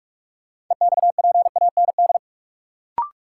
Morse code, heard as a keyed tone, about 46 wpm, 700 Hz. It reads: EXPAND
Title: Sentences from Top 1400 Words - New Word Drill - 45wpm